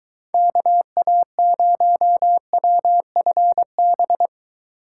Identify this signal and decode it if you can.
Morse code, heard as a keyed tone, about 23 wpm, 700 Hz. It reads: KA0WFB